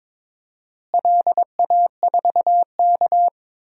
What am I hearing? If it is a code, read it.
Morse code, heard as a keyed tone, about 22 wpm, 700 Hz. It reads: LA4K
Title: DXCC-100 Call Signs - 4 Characters - Repeated 3x - 22wpm